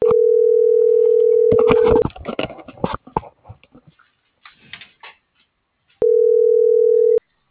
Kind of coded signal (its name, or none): telephone call progress tone